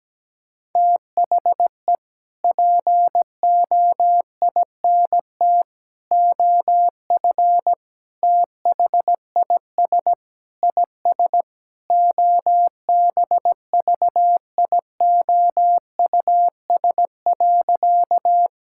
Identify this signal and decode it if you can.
Morse code, heard as a keyed tone, about 17 wpm, 700 Hz. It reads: THE POINT OF THIS IS OBVIOUS.
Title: Sentences from Top 1600 Words 17wpm